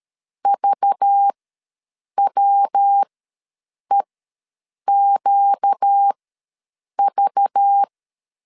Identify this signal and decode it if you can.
Morse code, heard as a keyed tone, about 13 wpm, spaced as for 7 wpm, 800 Hz. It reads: VWEQV
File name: LEZIONI / 5 LBVC / 5 LBV ws 12wpm 3015